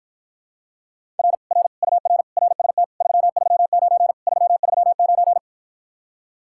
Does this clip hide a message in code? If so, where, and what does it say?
Yes: Morse code from 1.2 to 5.4 s, heard as a keyed tone, about 54 wpm, 700 Hz. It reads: R R UR RST 339 339